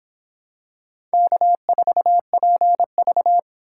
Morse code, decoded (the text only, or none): K4PV